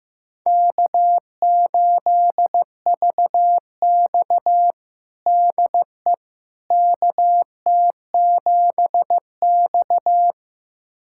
Morse code, heard as a keyed tone, about 16 wpm, 700 Hz. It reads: K8VX DE KT7X
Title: USA Callsign DE Callsign - 4 Characters - 15wpm